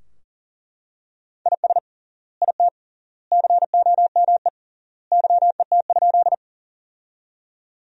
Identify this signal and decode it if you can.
Morse code, heard as a keyed tone, about 40 wpm, 700 Hz. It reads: IS IT COME YET?
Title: 3x Word-Spacing - Sentences from Top 500 Words - 40wpm